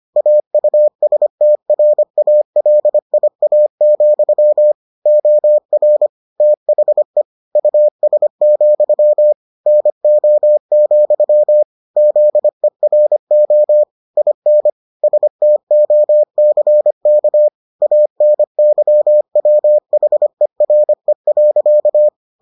Morse code, heard as a keyed tone, about 25 wpm, 600 Hz. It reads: AUSTRALIA, OR THE US, NO, ZERO IN STOCK ANYWHERE.